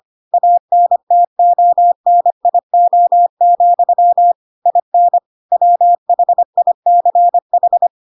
Morse code, heard as a keyed tone, about 25 wpm, 700 Hz. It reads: ANTONIO, IN WHICH
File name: Pinocchio / Pinocchio ch2 25 25 ik1hge